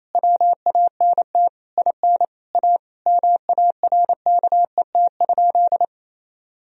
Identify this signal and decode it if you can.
Morse code, heard as a keyed tone, about 28 wpm, 700 Hz. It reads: WANT IN A MARKET?
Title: Sentences from Top 1000 Words - Review Part 3 - 28wpm